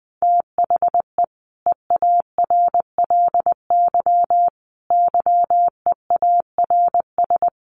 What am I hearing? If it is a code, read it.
Morse code, heard as a keyed tone, about 19 wpm, 700 Hz. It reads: THE EARLY YEARS
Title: Sets of 5 Words Encore 20wpm